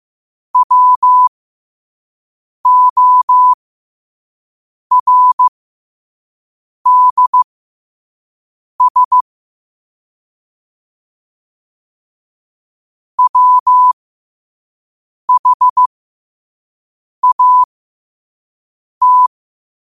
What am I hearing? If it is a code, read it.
Morse code, heard as a keyed tone, about 15 wpm, spaced as for 5 wpm, 1000 Hz. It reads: WORDS WHAT
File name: qotdfiles / 5wpm